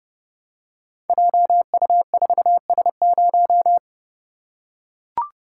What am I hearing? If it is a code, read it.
Morse code, heard as a keyed tone, about 30 wpm, 700 Hz. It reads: JU4S0
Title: Code Groups - 5 Letter:Numbers - 30wpm